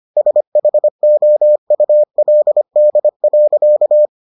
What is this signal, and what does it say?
Morse code, heard as a keyed tone, about 25 wpm, 600 Hz. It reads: SHOULD.